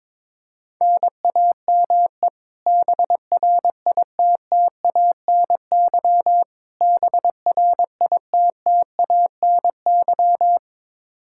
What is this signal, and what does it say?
Morse code, heard as a keyed tone, about 22 wpm, 700 Hz. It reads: NAME BRITTANY BRITTANY
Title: Practice QSO - Exchange 1 - Beginning - 22wpm